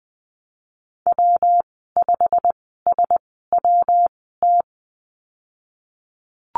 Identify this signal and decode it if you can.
Morse code, heard as a keyed tone, about 20 wpm, spaced as for 14 wpm, 700 Hz. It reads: W5SWT
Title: 122 Lesson 20g - Number 3 - Call Signs 20wpm